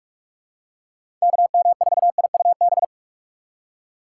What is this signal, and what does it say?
Morse code, heard as a keyed tone, about 45 wpm, 700 Hz. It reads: KM4IUB
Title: USA Call Signs - 6x Character Spacing - 45wpm